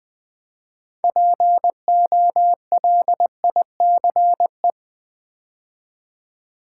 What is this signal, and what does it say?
Morse code, heard as a keyed tone, about 20 wpm, 700 Hz. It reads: POLICE
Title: Build A Word - 3 Letters At A Time - 6-Letter Words - 20wpm